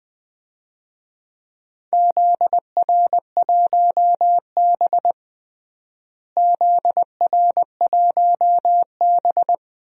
Morse code, heard as a keyed tone, about 20 wpm, 700 Hz. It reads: ZR1B ZR1B
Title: DXCC-100 Call Signs - 4 Characters - Repeated 2x - 20wpm